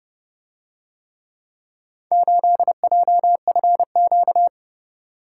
Morse code, heard as a keyed tone, about 29 wpm, 700 Hz. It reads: 8JFQ